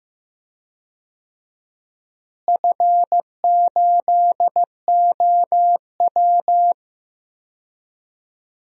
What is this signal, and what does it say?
Morse code, heard as a keyed tone, about 15 wpm, 700 Hz. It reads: F8OW
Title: DXCC-100 Call Signs - 4 Characters - Repeated 3x - 15wpm